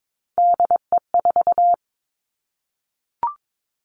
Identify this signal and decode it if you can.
Morse code, heard as a keyed tone, about 22 wpm, 700 Hz. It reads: DE4
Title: Three Challenging Letter-Number Combinations 22wpm